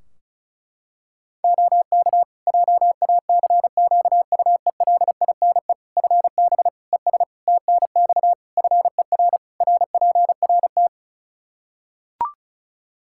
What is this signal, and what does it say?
Morse code, heard as a keyed tone, about 36 wpm, 700 Hz. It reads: OK JACQUELINE FB ES TNX FER RPRT
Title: K1USN SST QSO Practice 35wpm